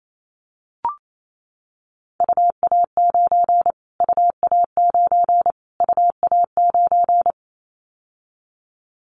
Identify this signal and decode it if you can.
Morse code, heard as a keyed tone, about 28 wpm, 700 Hz. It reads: UA9 UA9 UA9
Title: Callsign Prefixes for Top 25 DXCC - Repeated 3x - 28wpm